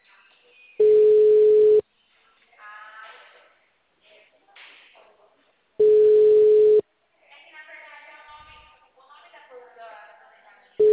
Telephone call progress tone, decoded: ringback tone

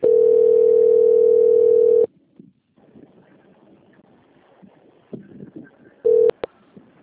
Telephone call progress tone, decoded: ringback tone

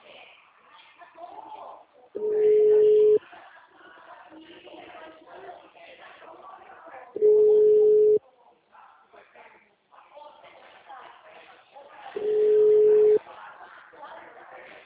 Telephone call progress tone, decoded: ringback tone

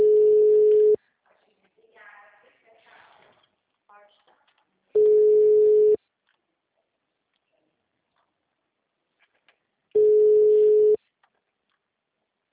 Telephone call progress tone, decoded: ringback tone

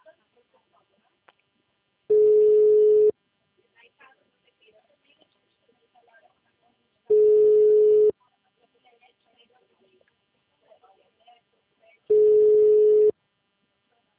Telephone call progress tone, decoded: ringback tone

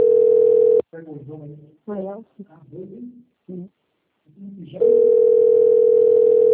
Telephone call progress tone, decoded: ringback tone

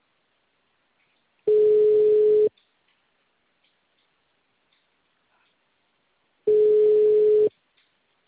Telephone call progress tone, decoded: ringback tone

